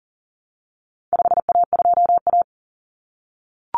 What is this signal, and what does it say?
Morse code, heard as a keyed tone, about 40 wpm, 700 Hz. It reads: 5A2A
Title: Common Cut Numbers - 4-Digit 40wpm